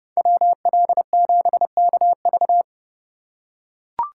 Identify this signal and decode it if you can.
Morse code, heard as a keyed tone, about 29 wpm, 700 Hz. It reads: WL7KV